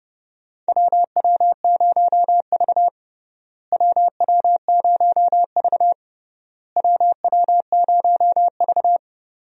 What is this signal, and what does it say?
Morse code, heard as a keyed tone, about 28 wpm, 700 Hz. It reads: WW0V WW0V WW0V
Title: USA Call Signs - 4 Characters - Repeated 3x - 30wpm